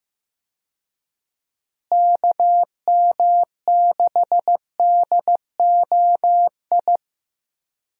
Morse code, heard as a keyed tone, about 15 wpm, 700 Hz. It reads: KM6DOI
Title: USA Call Signs - 6 Characters - Repeated 3x - 15wpm